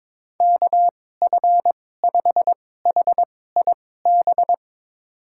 Morse code, heard as a keyed tone, about 22 wpm, spaced as for 16 wpm, 700 Hz. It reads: KF5HIB
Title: USA Call Signs - 6 Characters - 2x Character Spacing - 22wpm